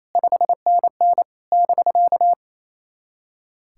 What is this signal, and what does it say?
Morse code, heard as a keyed tone, about 28 wpm, 700 Hz. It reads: 5NN <BK>
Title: SOTA QSO Practice 28wpm